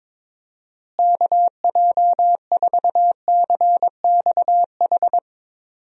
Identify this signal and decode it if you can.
Morse code, heard as a keyed tone, about 22 wpm, 700 Hz. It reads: KJ4CXH
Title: USA Call Signs - 6 Characters - Repeated 3x - 22wpm